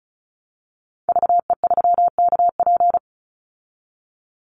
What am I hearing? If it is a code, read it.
Morse code, heard as a keyed tone, about 34 wpm, 700 Hz. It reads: VE3KP
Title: Call Signs - 35wpm